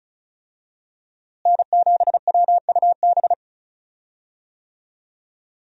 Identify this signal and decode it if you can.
Morse code, heard as a keyed tone, about 34 wpm, 700 Hz. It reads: N7WUB
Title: USA Call Signs - 5 Characters - Repeated 2x - 35wpm